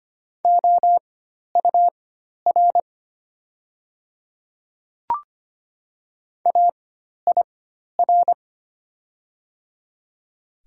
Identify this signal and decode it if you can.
Morse code, heard as a keyed tone, about 25 wpm, spaced as for 12 wpm, 700 Hz. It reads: OUR AIR
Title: 4x Character Spacing - Top 200 Words - 25wpm